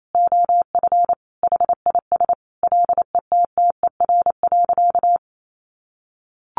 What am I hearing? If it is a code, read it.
Morse code, heard as a keyed tone, about 28 wpm, 700 Hz. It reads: OF HIS LETTER.
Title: Sentences from Top 1500 Words 28wpm